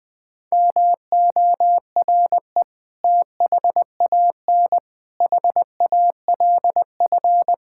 Morse code, heard as a keyed tone, about 20 wpm, 700 Hz. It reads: MORE THAN HALF
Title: Sentences with 5-Word Phrases - 20wpm